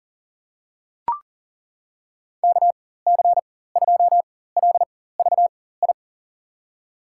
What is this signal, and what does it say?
Morse code, heard as a keyed tone, about 40 wpm, spaced as for 19 wpm, 700 Hz. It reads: KC2LVI
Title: USA Call Signs - 6 Characters - 4x Character Spacing - 40wpm